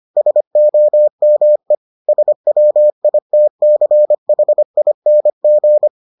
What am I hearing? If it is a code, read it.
Morse code, heard as a keyed tone, about 25 wpm, 600 Hz. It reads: SOME SWITCHING